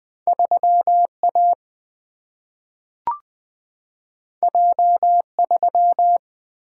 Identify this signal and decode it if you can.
Morse code, heard as a keyed tone, about 20 wpm, 700 Hz. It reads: 3A J3
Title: Single-Letter Single-Number 20wpm